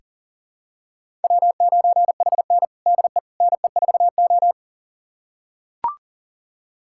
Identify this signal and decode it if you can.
Morse code, heard as a keyed tone, about 40 wpm, 700 Hz. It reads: W9HN DE NE4O
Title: USA Callsign DE Callsign - 4 Characters - 40wpm